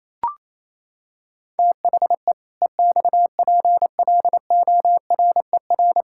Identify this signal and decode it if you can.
Morse code, heard as a keyed tone, about 28 wpm, 700 Hz. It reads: THE EXPLORER
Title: Pangram Challenge v2 - 28wpm